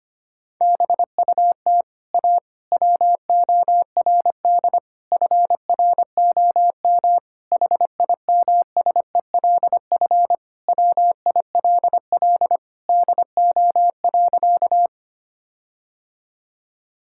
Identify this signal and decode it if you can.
Morse code, heard as a keyed tone, about 25 wpm, 700 Hz. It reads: BUT A WORD FROM HIMSELF WILL DO.
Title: Sentences from Top 1600 Words 25wpm